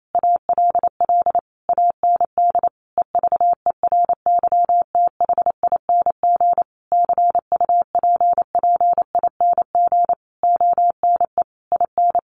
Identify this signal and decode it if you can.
Morse code, heard as a keyed tone, about 28 wpm, 700 Hz. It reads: ALL AND EVERYTHING CUPPING ONE IN
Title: A Sound Of Thunder 28wpm